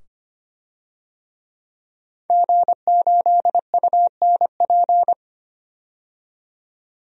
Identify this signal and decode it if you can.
Morse code, heard as a keyed tone, about 25 wpm, 700 Hz. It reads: G8UNP
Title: DXCC-100 Call Signs - 5 Characters - 25wpm